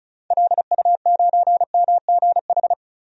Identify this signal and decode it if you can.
Morse code, heard as a keyed tone, about 36 wpm, 700 Hz. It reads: LU9MGH
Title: DXCC-100 Call Signs - 6 Characters - 35wpm